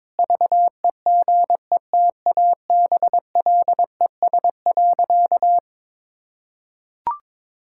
Morse code, heard as a keyed tone, about 22 wpm, 700 Hz. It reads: VEGETABLES.
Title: Pangram Challenge v2 - 22wpm